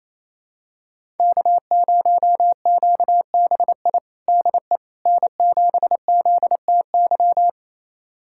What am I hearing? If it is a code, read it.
Morse code, heard as a keyed tone, about 28 wpm, 700 Hz. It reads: K0QBI DE N7ZTY
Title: USA Callsign DE Callsign - 5 Characters - 28wpm